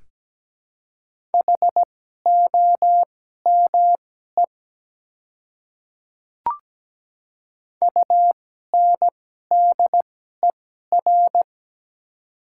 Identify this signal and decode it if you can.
Morse code, heard as a keyed tone, about 17 wpm, spaced as for 12 wpm, 700 Hz. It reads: HOME UNDER
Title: 2x Character Spacing - Top 200 Words - 17wpm